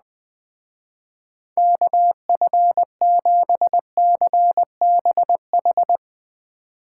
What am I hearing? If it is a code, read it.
Morse code, heard as a keyed tone, about 20 wpm, 700 Hz. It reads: KF7CBH
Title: Speed Racing - USA Call Signs 17wpm